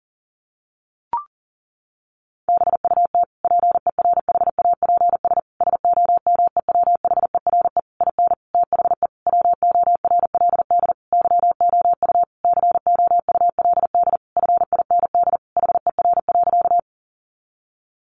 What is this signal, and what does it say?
Morse code, heard as a keyed tone, about 40 wpm, 700 Hz. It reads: BUT PERHAPS SOMEWHERE IN THE WORLD YOU COULD FIND HER.
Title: Sentences from Top 1400 Words 40wpm